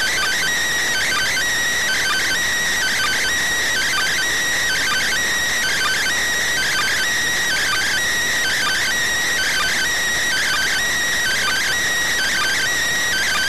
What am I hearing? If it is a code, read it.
Two-tone teleprinter signal, about 400 Hz shift.